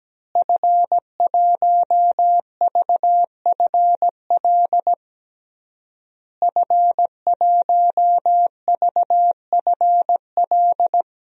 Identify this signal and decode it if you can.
Morse code, heard as a keyed tone, about 17 wpm, 700 Hz. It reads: F1VFL F1VFL
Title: DXCC-100 Call Signs - 5 Characters - Repeated 3x - 17wpm